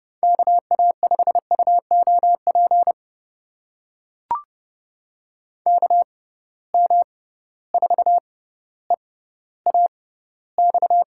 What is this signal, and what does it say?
Morse code, heard as a keyed tone, about 32 wpm, spaced as for 16 wpm, 700 Hz. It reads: KA5UOP KM4EAX
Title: USA Call Signs - 6 Characters - 6x Character Spacing - 30wpm